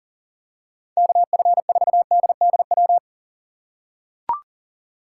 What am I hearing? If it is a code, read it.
Morse code, heard as a keyed tone, about 37 wpm, 700 Hz. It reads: KF4DDW
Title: USA Call Signs - 6 Characters - 4x Character Spacing - 40wpm